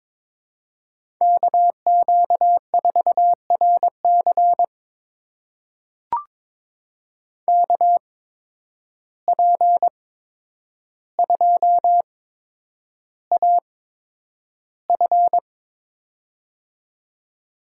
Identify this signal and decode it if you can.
Morse code, heard as a keyed tone, about 22 wpm, spaced as for 9 wpm, 700 Hz. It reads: KQ4RC KP2AF
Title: USA Call Signs - 5 Characters - 8x Character Spacing - 22wpm